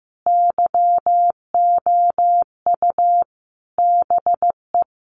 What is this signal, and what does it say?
Morse code, heard as a keyed tone, about 14 wpm, 700 Hz. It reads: YOU BE
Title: Sentences from Top 1400 Words 15wpm